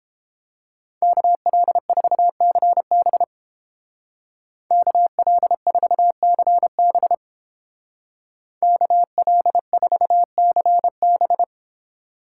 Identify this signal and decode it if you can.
Morse code, heard as a keyed tone, about 29 wpm, 700 Hz. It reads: KL4CB KL4CB KL4CB